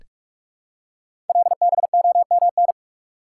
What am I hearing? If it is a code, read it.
Morse code, heard as a keyed tone, about 44 wpm, 700 Hz. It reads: LBOMN
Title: Code Groups - 5 Letters - 45wpm